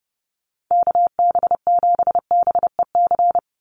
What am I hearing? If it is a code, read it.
Morse code, heard as a keyed tone, about 30 wpm, 700 Hz. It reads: KB7BEC
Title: USA Call Signs 30wpm